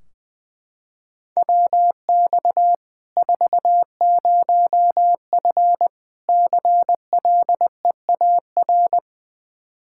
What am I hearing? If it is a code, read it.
Morse code, heard as a keyed tone, about 20 wpm, 700 Hz. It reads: WX 40F CLEAR
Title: Practice QSO - Exchange 2 - Info - 20wpm